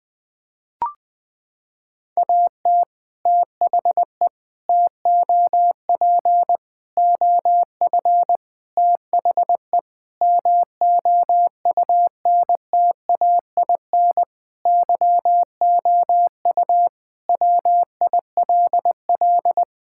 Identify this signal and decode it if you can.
Morse code, heard as a keyed tone, about 20 wpm, 700 Hz. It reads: AT THE TOP OF THE MOUNTAIN YOU WILL
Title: Sentences with 3-Word Phrases - 20wpm